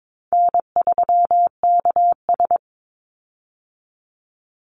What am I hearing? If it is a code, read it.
Morse code, heard as a keyed tone, about 22 wpm, 700 Hz. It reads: N3KS
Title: Easy Call Signs Repeated 3x - 22wpm